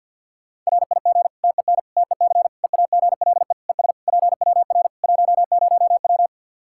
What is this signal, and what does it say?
Morse code, heard as a keyed tone, about 52 wpm, 700 Hz. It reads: RIG TEN TEC EAGLE ES PWR 10W
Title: K1USN SST QSO Practice 50wpm